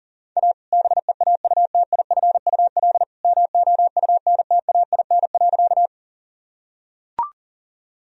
Morse code, heard as a keyed tone, about 40 wpm, 700 Hz. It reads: A BEAUTIFUL MOUNTAIN.
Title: Sentences from Top 1000 Words - Review Part 1 - 40wpm